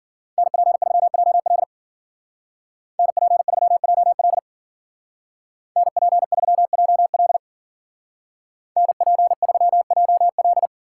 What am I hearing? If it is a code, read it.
Morse code, heard as a keyed tone, about 50 wpm, 700 Hz. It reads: NP3JL NP3JL NP3JL NP3JL
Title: Speed Racing - USA 2x2 Call Signs 40wpm